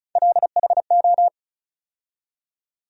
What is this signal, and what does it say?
Morse code, heard as a keyed tone, about 36 wpm, 700 Hz. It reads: LHO
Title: USA Call Sign Postfixes - 3 Letters 35wpm